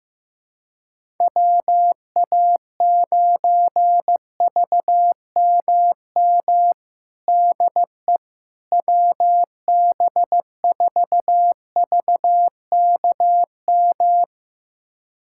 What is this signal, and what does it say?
Morse code, heard as a keyed tone, about 15 wpm, 700 Hz. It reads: WA9VMM DE WB4VKM
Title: USA Callsign DE Callsign - 6 Characters - 15wpm